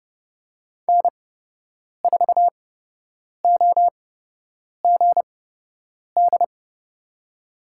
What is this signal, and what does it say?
Morse code, heard as a keyed tone, about 30 wpm, spaced as for 8 wpm, 700 Hz. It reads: N4OGD
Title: USA Call Signs - 5 Characters - 8x Character Spacing - 30wpm